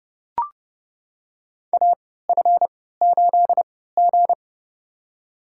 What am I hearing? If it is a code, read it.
Morse code, heard as a keyed tone, about 31 wpm, spaced as for 17 wpm, 700 Hz. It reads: AF8G
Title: USA Call Signs - 4 Characters - 3x Character Spacing - 30wpm